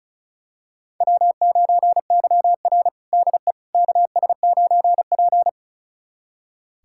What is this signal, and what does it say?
Morse code, heard as a keyed tone, about 34 wpm, 700 Hz. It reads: W9YR DE KS9P
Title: USA Callsign DE Callsign - 4 Characters - 35wpm